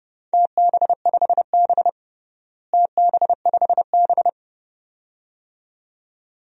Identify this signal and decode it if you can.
Morse code, heard as a keyed tone, about 30 wpm, 700 Hz. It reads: TB5B TB5B
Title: DXCC-100 Call Signs - 4 Characters - Repeated 2x - 30wpm